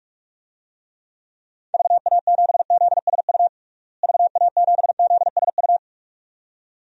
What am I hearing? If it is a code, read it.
Morse code, heard as a keyed tone, about 46 wpm, 700 Hz. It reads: VA7ZSU VA7ZSU
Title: DXCC-100 Call Signs - 6 Characters - Repeated 2x - 45wpm